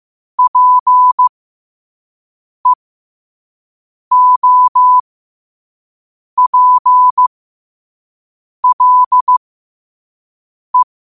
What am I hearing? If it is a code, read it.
Morse code, heard as a keyed tone, about 15 wpm, spaced as for 5 wpm, 1000 Hz. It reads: PEOPLE
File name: qotdfiles / 5wpm